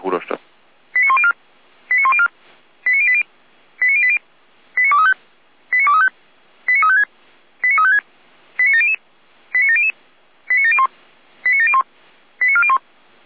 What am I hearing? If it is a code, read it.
Five-tone selective call (ZVEI), heard as quick runs of five tones, tones of about 70 ms, with speech. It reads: 89104 89104 89080 89080 89235 89235 89356 89356 89700 89700 89701 89701 89401